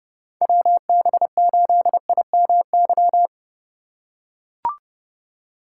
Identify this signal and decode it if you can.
Morse code, heard as a keyed tone, about 32 wpm, 700 Hz. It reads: WB8IMY
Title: K1USN SST QSO Practice - 30wpm